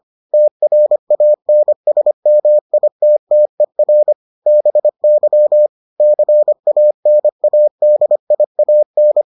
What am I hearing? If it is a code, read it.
Morse code, heard as a keyed tone, about 25 wpm, 600 Hz. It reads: TRANSMITTER BY CANADIAN